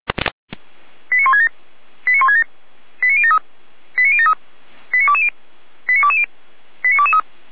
Five-tone selective call (ZVEI), heard as quick runs of five tones, tones of about 70 ms. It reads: 89167 89167 89073 89073 89220 89220 89223